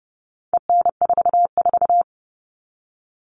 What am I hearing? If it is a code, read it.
Morse code, heard as a keyed tone, about 30 wpm, 700 Hz. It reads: EN44
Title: All Cut Numbers - 4-Digit 30wpm